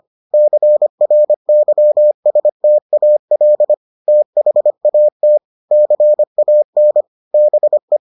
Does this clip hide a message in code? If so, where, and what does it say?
Yes: Morse code from 0.3 to 8.0 s, heard as a keyed tone, about 25 wpm, 600 Hz. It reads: CRYSTAL THAT CAN BE